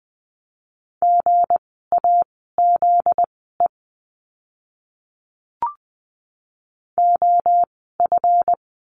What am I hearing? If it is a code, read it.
Morse code, heard as a keyed tone, about 20 wpm, spaced as for 14 wpm, 700 Hz. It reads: GAZE OF
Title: 244 Lesson 39f - Letter z - Words 20wpm